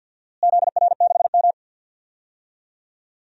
Morse code, heard as a keyed tone, about 46 wpm, 700 Hz. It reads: ZR6M
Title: DXCC-100 Call Signs - 4 Characters - Repeated 2x - 50wpm